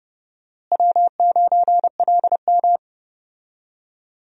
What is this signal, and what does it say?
Morse code, heard as a keyed tone, about 30 wpm, 700 Hz. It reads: W9LM
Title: USA Call Signs - 4 Characters - Repeated 2x - 30wpm